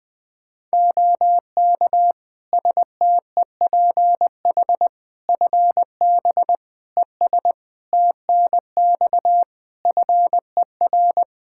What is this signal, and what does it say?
Morse code, heard as a keyed tone, about 20 wpm, 700 Hz. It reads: OK STEPH FB ES TNX FER
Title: Practice QSO - Exchange 2 - Info - 20wpm